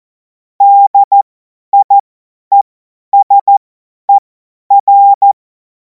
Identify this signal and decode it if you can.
Morse code, heard as a keyed tone, about 14 wpm, spaced as for 10 wpm, 800 Hz. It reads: DIESER
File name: Morsekurs / Witze / Witz 10. März 2026 50 F70